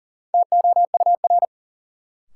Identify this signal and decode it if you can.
Morse code, heard as a keyed tone, about 40 wpm, 700 Hz. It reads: TOUR
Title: Sentences from Top 1400 Words - New Word Drill Alt - 40wpm